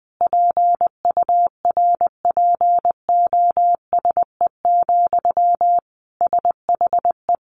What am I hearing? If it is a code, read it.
Morse code, heard as a keyed tone, about 20 wpm, 700 Hz. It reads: PURPOSE, SHE
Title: Sentences from Top 800 Words 20wpm